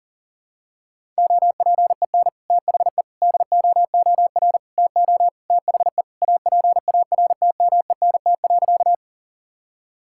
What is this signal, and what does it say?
Morse code, heard as a keyed tone, about 37 wpm, 700 Hz. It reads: OPEN THE DOOR TO THE APARTMENT.
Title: Sentences from Top 1000 Words - Review Part 6 - 40wpm